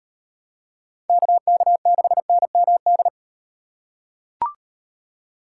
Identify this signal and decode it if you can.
Morse code, heard as a keyed tone, about 37 wpm, 700 Hz. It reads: KK6NMD